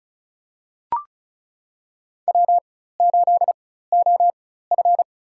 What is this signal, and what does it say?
Morse code, heard as a keyed tone, about 36 wpm, spaced as for 16 wpm, 700 Hz. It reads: W8OF
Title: USA Call Signs - 4 Characters - 4x Character Spacing - 35wpm